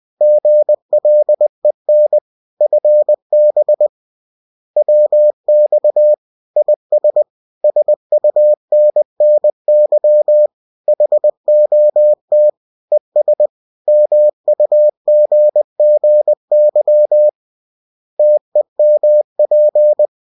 Morse code, heard as a keyed tone, about 20 wpm, 600 Hz. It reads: GLEN FB WX IS SUNNY HOT ES MUGGY TEMP